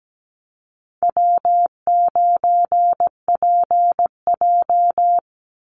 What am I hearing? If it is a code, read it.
Morse code, heard as a keyed tone, about 17 wpm, 700 Hz. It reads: W9PJ
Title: Easy Call Signs - 17wpm